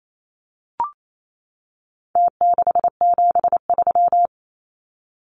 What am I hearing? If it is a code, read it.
Morse code, heard as a keyed tone, about 28 wpm, 700 Hz. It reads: T673
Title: Numbers - Contest Style - 4-Digit 28wpm